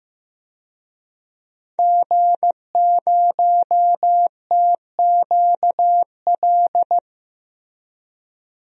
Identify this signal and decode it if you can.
Morse code, heard as a keyed tone, about 15 wpm, 700 Hz. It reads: G0TQL
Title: DXCC-100 Call Signs - 5 Characters - 15wpm